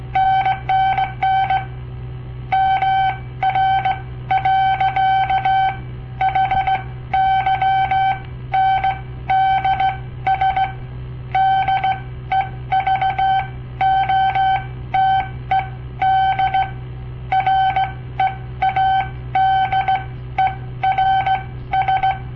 Morse code, heard as a keyed tone, about 17 wpm, spaced as for 13 wpm, 770 Hz. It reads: NNN MR. HYNDS DEVOTED READERS
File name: papertapes / local / Code Practice-01